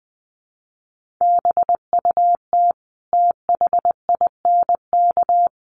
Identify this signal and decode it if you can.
Morse code, heard as a keyed tone, about 20 wpm, 700 Hz. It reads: BUT THINK
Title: Sentences from Top 1100 Words 20wpm